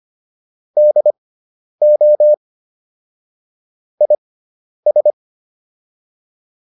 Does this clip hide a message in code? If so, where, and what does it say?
Yes: Morse code from 0.8 to 5.1 s, heard as a keyed tone, about 25 wpm, spaced as for 10 wpm, 600 Hz. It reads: DO IS